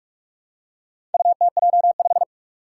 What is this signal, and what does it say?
Morse code, heard as a keyed tone, about 44 wpm, 700 Hz. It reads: UTJ5
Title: Four Challenging Letter-Number Combinations 45wpm